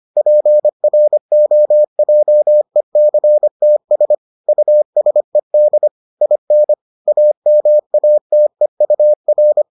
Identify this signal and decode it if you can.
Morse code, heard as a keyed tone, about 25 wpm, 600 Hz. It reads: PROJECTS USED IN AMATEUR